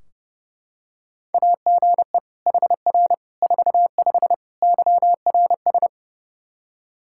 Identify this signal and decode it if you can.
Morse code, heard as a keyed tone, about 30 wpm, 700 Hz. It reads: AGE HR 45 YRS
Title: Practice QSO - Exchange 3 - More Info - 30wpm